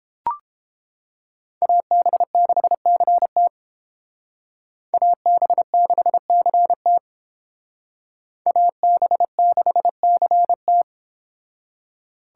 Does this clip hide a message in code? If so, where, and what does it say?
Yes: Morse code from 1.6 to 10.8 s, heard as a keyed tone, about 30 wpm, 700 Hz. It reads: AB6CT AB6CT AB6CT